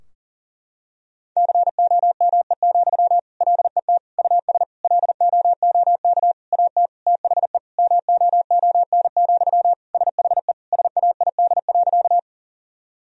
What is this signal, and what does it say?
Morse code, heard as a keyed tone, about 40 wpm, 700 Hz. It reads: COME, LET US LOOK AT THE MOON, SHE SAID.
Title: Sentences from Top 1000 Words - Review Part 3 - 40wpm